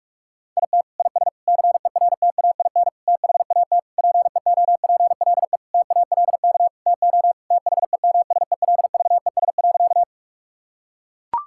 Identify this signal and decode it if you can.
Morse code, heard as a keyed tone, about 41 wpm, 700 Hz. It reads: IT IS CERTAIN THAT PEOPLE TALK TO THEMSELVES.
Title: Sentences from Top 1000 Words - Review Part 5 - 45wpm